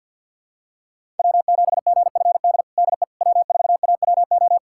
Morse code, heard as a keyed tone, about 53 wpm, 700 Hz. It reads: W7GFD DE W4AWO
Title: USA Callsign DE Callsign - 5 Characters - 50wpm